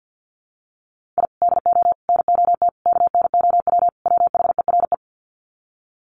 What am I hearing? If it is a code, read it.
Morse code, heard as a keyed tone, about 50 wpm, 700 Hz. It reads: I DO NOT KNOW WHERE
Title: Sets of 5 Words 50wpm